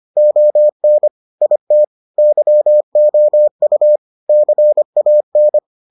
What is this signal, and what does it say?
Morse code, heard as a keyed tone, about 25 wpm, 600 Hz. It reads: ON IT YOU CAN